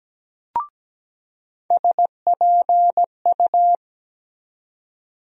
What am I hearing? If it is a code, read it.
Morse code, heard as a keyed tone, about 17 wpm, 700 Hz. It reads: SPU
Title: USA Call Sign Postfixes - 3 Letters 17wpm